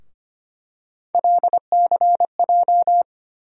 Morse code, heard as a keyed tone, about 25 wpm, 700 Hz. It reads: LCJ
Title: USA Call Sign Postfixes - 3 Letters 25wpm